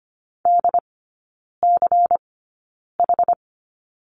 Morse code, heard as a keyed tone, about 25 wpm, spaced as for 9 wpm, 700 Hz. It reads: DCH
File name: course / 25:10wpm / 093 Lesson 16e - Letter c - Three Characters 25wpm